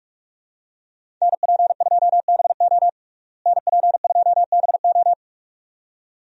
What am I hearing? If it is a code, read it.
Morse code, heard as a keyed tone, about 46 wpm, 700 Hz. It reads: NP2BO NP2BO
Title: USA Call Signs - 5 Characters - Repeated 2x - 45wpm